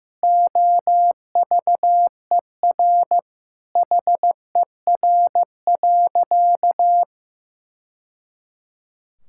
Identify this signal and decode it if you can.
Morse code, heard as a keyed tone, about 15 wpm, 700 Hz. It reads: OVER HER.
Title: Sentences from Top 1000 Words - Review Part 5 - 15wpm